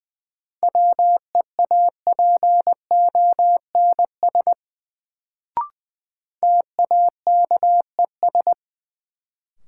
Morse code, heard as a keyed tone, about 20 wpm, 700 Hz. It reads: WEAPONS TAKES
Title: Sets of 5 Words Encore - New Word Drill - 20wpm